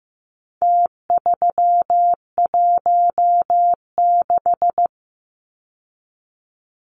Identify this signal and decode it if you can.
Morse code, heard as a keyed tone, about 15 wpm, 700 Hz. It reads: T316
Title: Numbers - Contest Style - 4-Digit 15wpm